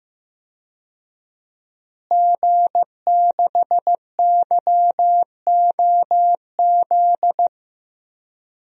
Morse code, heard as a keyed tone, about 15 wpm, 700 Hz. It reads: G6YOZ
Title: DXCC-100 Call Signs - 5 Characters - Repeated 2x - 15wpm